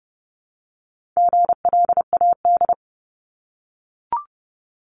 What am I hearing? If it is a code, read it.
Morse code, heard as a keyed tone, about 30 wpm, 700 Hz. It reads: GLAD
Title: Multi-letter Phonemes 30wpm